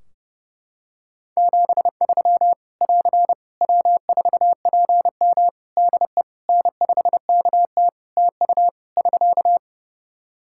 Morse code, heard as a keyed tone, about 30 wpm, 700 Hz. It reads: 73 <AR> W4PM DE N5KT TU <SK>
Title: K1USN SST QSO Practice 30wpm